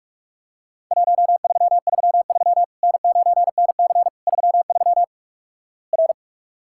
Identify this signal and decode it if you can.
Morse code, heard as a keyed tone, about 48 wpm, 700 Hz. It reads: 1333 N9NC 33 R
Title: Worked All Europe DX Contest - QTC Exchange - 45wpm